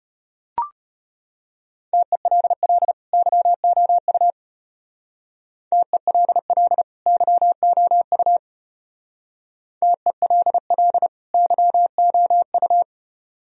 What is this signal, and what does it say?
Morse code, heard as a keyed tone, about 33 wpm, 700 Hz. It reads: TELL YOU TELL YOU TELL YOU